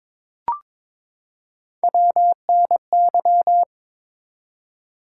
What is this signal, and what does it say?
Morse code, heard as a keyed tone, about 22 wpm, 700 Hz. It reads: WNY
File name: ARRL Sections v2 - 22wpm